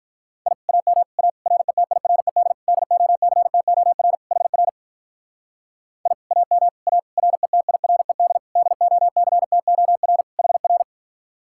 Morse code, heard as a keyed tone, about 42 wpm, 700 Hz. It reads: I AM A RETIRED DOCTOR HR I AM A RETIRED DOCTOR HR
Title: Speed-Racing Common QSO Phrases Encore - 35wpm